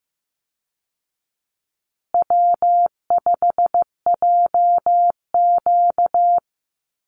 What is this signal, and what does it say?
Morse code, heard as a keyed tone, about 15 wpm, 700 Hz. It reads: W5JQ
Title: Call Signs Repeated 3x - 15wpm